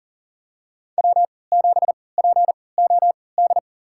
Morse code, heard as a keyed tone, about 40 wpm, spaced as for 23 wpm, 700 Hz. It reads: W7POD
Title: USA Call Signs - 3x Character Spacing - 40wpm